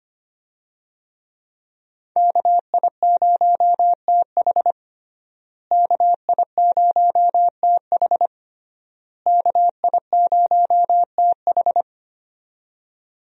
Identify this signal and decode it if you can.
Morse code, heard as a keyed tone, about 25 wpm, 700 Hz. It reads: KI0TH KI0TH KI0TH
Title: USA Call Signs - 5 Characters - Repeated 3x - 25wpm